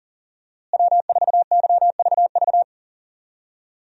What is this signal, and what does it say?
Morse code, heard as a keyed tone, about 40 wpm, 700 Hz. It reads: W4YVV
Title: USA Call Signs - 5 Characters - Repeated 2x - 40wpm